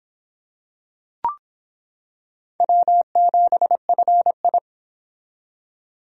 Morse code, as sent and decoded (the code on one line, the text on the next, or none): .-- --... ..-. ..
W7FI